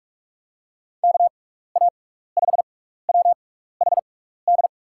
Morse code, heard as a keyed tone, about 46 wpm, 700 Hz. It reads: K A 5 W H D
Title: USA Call Signs - 6 Characters - 6x Character Spacing - 45wpm